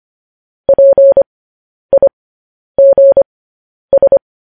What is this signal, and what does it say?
Morse code, heard as a keyed tone, about 25 wpm, spaced as for 10 wpm, 550 Hz. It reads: PIGS
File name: podcast / News headlines-1x-25x10